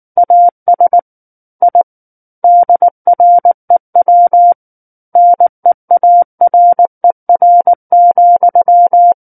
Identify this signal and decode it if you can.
Morse code, heard as a keyed tone, about 19 wpm, 700 Hz. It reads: AS I DREW NEARER,
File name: Book One / 10-Chapter 10 - 19 WPM